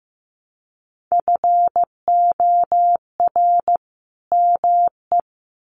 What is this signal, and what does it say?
Morse code, heard as a keyed tone, about 15 wpm, 700 Hz. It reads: FOR ME
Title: Sets of 2 Words 15wpm